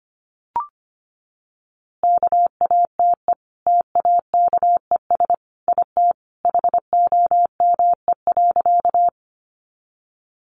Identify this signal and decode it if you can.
Morse code, heard as a keyed tone, about 25 wpm, 700 Hz. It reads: KATE TAKES IT HOME.
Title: Easy Phrases 25wpm